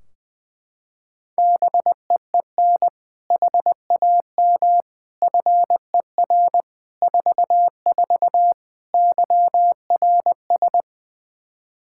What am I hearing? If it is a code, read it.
Morse code, heard as a keyed tone, about 20 wpm, 700 Hz. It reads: BEEN HAM FER 44 YRS
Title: Practice QSO - Exchange 3 - More Info - 20wpm